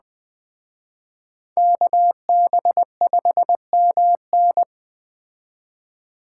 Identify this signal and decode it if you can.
Morse code, heard as a keyed tone, about 20 wpm, 700 Hz. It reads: KB5MN
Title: Speed Racing - USA 2x2 Call Signs 17wpm